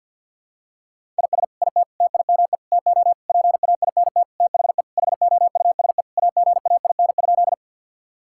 Morse code, heard as a keyed tone, about 47 wpm, 700 Hz. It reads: IS IT TIME TO PAINT THE HOUSE AGAIN?
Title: Sentences from Top 1000 Words - Review Part 6 - 50wpm